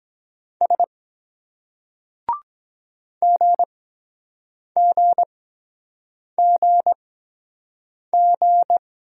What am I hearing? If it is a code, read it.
Morse code, heard as a keyed tone, about 23 wpm, spaced as for 7 wpm, 700 Hz. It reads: S GGGG